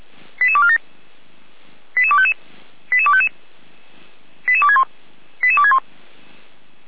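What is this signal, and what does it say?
Five-tone selective call (ZVEI), heard as quick runs of five tones, tones of about 70 ms. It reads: 88247 88255 88255 88261 88261
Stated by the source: ZVEI five-tone selective call received on an SDR.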